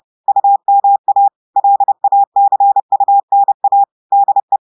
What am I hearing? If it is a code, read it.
Morse code, heard as a keyed tone, about 30 wpm, 800 Hz. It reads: UMA LACUNA DE
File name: audios / portugais / portugais texte en clair 1 vitesse 30 mots minutes